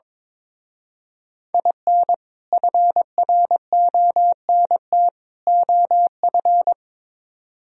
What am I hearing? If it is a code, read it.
Morse code, heard as a keyed tone, about 24 wpm, 700 Hz. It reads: IN FRONT OF
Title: Speed-Racing - Sets of 3 Words 22wpm